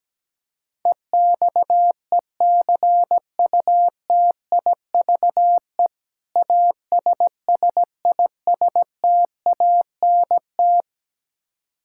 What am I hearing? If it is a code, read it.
Morse code, heard as a keyed tone, about 17 wpm, 700 Hz. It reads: EXECUTIVE ASSISTANT
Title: Practice QSO - Exchange 3 - More Info - 17wpm